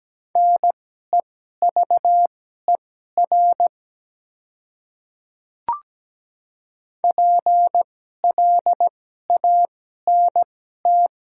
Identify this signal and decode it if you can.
Morse code, heard as a keyed tone, about 17 wpm, spaced as for 12 wpm, 690 Hz. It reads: NEVER PLANT
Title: 2x Character Spacing - Top 300 Words - 17wpm